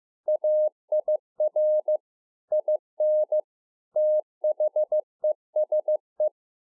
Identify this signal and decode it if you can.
Morse code, heard as a keyed tone, about 15 wpm, 610 Hz. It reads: AIR IN THESE
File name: 15WPM